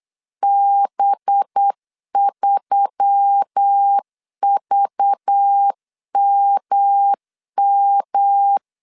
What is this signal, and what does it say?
Morse code, heard as a keyed tone, about 8 wpm, 800 Hz. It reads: B3VMM
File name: CW / DET / 1 6 1500C S B 8wpm 5142